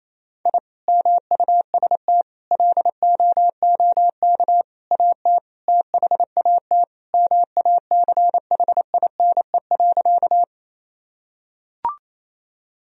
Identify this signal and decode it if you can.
Morse code, heard as a keyed tone, about 28 wpm, 700 Hz. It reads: I MUST LOOK AT THAT MACHINE.
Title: Sentences from Top 1000 Words - Review Part 2 - 28wpm